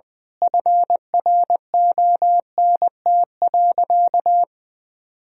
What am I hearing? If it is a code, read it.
Morse code, heard as a keyed tone, about 20 wpm, 700 Hz. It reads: FRONT.
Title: Sentences from Top 1000 Words - Review Part 2 - 20wpm